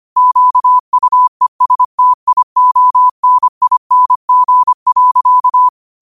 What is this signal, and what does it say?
Morse code, heard as a keyed tone, about 25 wpm, 1000 Hz. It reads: QUESTIONING.